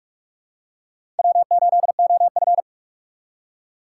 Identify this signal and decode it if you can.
Morse code, heard as a keyed tone, about 42 wpm, 700 Hz. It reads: W8OF
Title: USA Call Signs - 2x Character Spacing - 45wpm